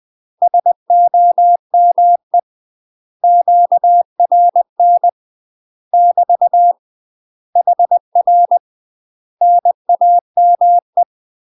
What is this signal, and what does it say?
Morse code, heard as a keyed tone, about 20 wpm, 700 Hz. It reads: SOME QRN <BT> HR NAME
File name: UK MASTER QSOs TRAINING / 20WPM / UK MASTER QSOs 0070